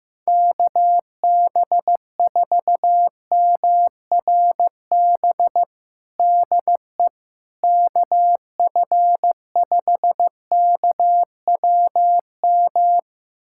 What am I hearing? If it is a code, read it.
Morse code, heard as a keyed tone, about 15 wpm, 700 Hz. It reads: KB4MRB DE KF5KWM
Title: USA Callsign DE Callsign 15wpm